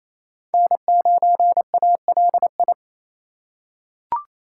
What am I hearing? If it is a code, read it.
Morse code, heard as a keyed tone, about 28 wpm, 700 Hz. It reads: N9ALI
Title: USA Call Signs - 5 Characters - 4x Character Spacing - 28wpm